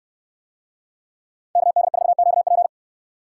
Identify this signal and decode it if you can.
Morse code, heard as a keyed tone, about 69 wpm, 700 Hz. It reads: KD4QP